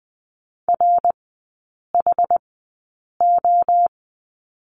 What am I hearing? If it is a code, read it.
Morse code, heard as a keyed tone, about 20 wpm, spaced as for 8 wpm, 700 Hz. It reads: RHO
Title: 087 Lesson 15e - Number 5 - Three Characters 20wpm